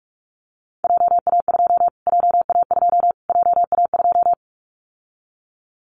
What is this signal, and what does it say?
Morse code, heard as a keyed tone, about 46 wpm, 700 Hz. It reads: JA2 JA2 JA2
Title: Callsign Prefixes for Top 25 DXCC - Repeated 3x - 45wpm